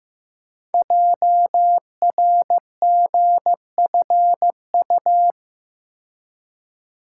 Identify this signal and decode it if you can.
Morse code, heard as a keyed tone, about 15 wpm, 700 Hz. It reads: JRGFU
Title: Code Groups - 5 Letters - 15wpm